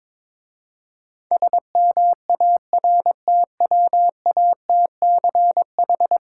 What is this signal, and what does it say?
Morse code, heard as a keyed tone, about 22 wpm, 700 Hz. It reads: SMARTWATCH
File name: Difficult Words with Hints - 22wpm